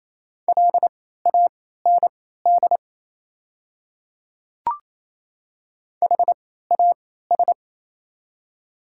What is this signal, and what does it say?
Morse code, heard as a keyed tone, about 28 wpm, spaced as for 16 wpm, 700 Hz. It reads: LAND HAS
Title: 3x Character Spacing - Top 300 Words - 28wpm